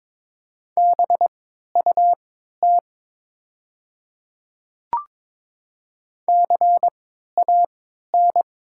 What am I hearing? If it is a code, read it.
Morse code, heard as a keyed tone, about 22 wpm, spaced as for 13 wpm, 700 Hz. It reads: BUT CAN
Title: 3x Character Spacing - Top 100 Words - 22wpm